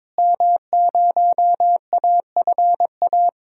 Morse code, heard as a keyed tone, about 22 wpm, 700 Hz. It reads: M0AFA
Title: DXCC-100 Call Signs - 5 Characters - Repeated 2x - 22wpm